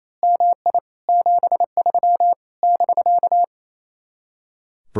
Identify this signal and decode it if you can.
Morse code, heard as a keyed tone, about 26 wpm, 700 Hz. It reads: MI 73 <BK>
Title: Speed-Racing Common QSO Phrases Encore - 28wpm